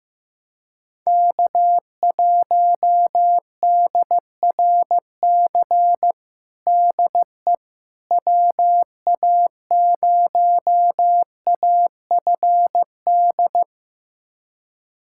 Morse code, heard as a keyed tone, about 15 wpm, 700 Hz. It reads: K1DRC DE WA0AFD
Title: USA Callsign DE Callsign 15wpm